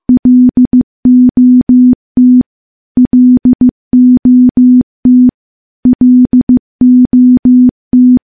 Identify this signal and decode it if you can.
Morse code, heard as a keyed tone, about 15 wpm, 250 Hz. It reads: LOT LOT LOT
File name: morse-training-15WPM